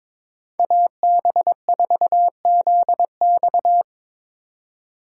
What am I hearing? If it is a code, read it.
Morse code, heard as a keyed tone, about 22 wpm, 700 Hz. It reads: AB4ZX
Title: USA Call Signs - 5 Characters - 2x Character Spacing - 22wpm